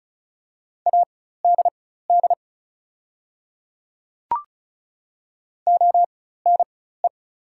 Morse code, heard as a keyed tone, about 36 wpm, spaced as for 16 wpm, 690 Hz. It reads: ADD ONE
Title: 4x Character Spacing - Top 500 Words - 35wpm